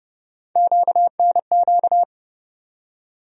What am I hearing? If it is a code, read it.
Morse code, heard as a keyed tone, about 31 wpm, 700 Hz. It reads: QNQ
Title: USA Call Sign Postfixes - 3 Letters 30wpm